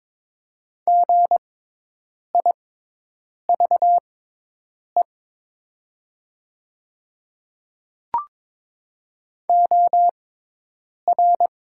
Morse code, heard as a keyed tone, about 22 wpm, spaced as for 8 wpm, 700 Hz. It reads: GIVE OR
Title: 6x Character Spacing - Top 200 Words - 22wpm